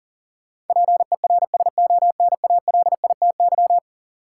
Morse code, heard as a keyed tone, about 40 wpm, 700 Hz. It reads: PERSONALITY
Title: Sentences from Top 1400 Words - New Word Drill Alt - 40wpm